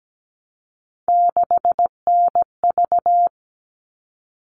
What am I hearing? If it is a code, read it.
Morse code, heard as a keyed tone, about 17 wpm, 700 Hz. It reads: 6NV
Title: All Cut Numbers - 3-Digit 17wpm